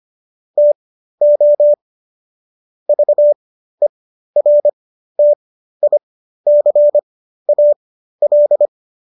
Morse code, heard as a keyed tone, about 25 wpm, spaced as for 13 wpm, 590 Hz. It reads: TO VERTICAL